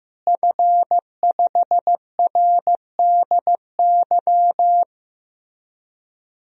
Morse code, heard as a keyed tone, about 15 wpm, 700 Hz. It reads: F5RDY
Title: DXCC-100 Call Signs - 5 Characters - Repeated 3x - 15wpm